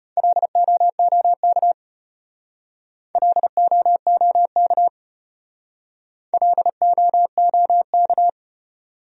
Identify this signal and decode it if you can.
Morse code, heard as a keyed tone, about 32 wpm, 700 Hz. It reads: LOOK LOOK LOOK